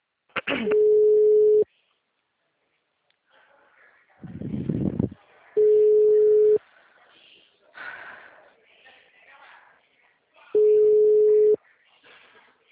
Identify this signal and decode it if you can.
Ringback tone (Europe, 425 Hz), with speech.
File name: out- -2003- - - .69